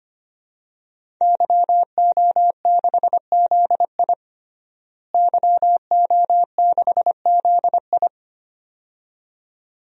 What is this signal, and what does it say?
Morse code, heard as a keyed tone, about 25 wpm, 700 Hz. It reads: YO6ZI YO6ZI
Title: DXCC-100 Call Signs - 5 Characters - Repeated 2x - 25wpm